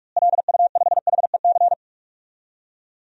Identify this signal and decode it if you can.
Morse code, heard as a keyed tone, about 47 wpm, 700 Hz. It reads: LU5HEC
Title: DXCC-100 Call Signs - 6 Characters - Repeated 2x - 45wpm